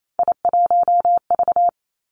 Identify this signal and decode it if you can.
Morse code, heard as a keyed tone, about 28 wpm, 700 Hz. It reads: I1V
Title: Three Challenging Letter-Number Combinations 28wpm